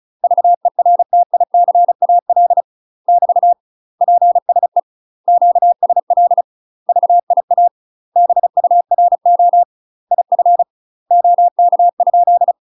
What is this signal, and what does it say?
Morse code, heard as a keyed tone, about 36 wpm, 700 Hz. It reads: VERTICAL <BT> PSE QSL VIA BURO IF OK?
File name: UK MASTER QSOs TRAINING / 35WPM / UK MASTER QSOs 0010